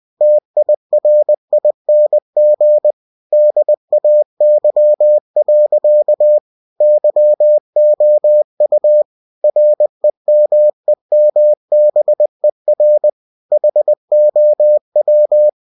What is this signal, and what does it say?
Morse code, heard as a keyed tone, about 20 wpm, 600 Hz. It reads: TIRING DAY. YOU REMEMBER HOW
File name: cw-books / winnie-the-pooh / winnie-20wpm-page-0093